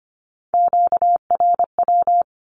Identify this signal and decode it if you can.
Morse code, heard as a keyed tone, about 25 wpm, 700 Hz. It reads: QRW
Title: Q-Codes Repeated 3x 25wpm